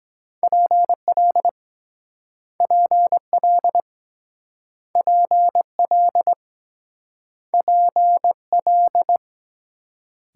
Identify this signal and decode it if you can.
Morse code, heard as a keyed tone, about 22 wpm, 700 Hz. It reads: PL PL PL PL